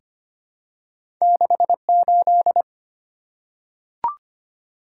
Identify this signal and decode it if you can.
Morse code, heard as a keyed tone, about 25 wpm, 700 Hz. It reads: 68
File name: Two Letter-Number Combinations - 25wpm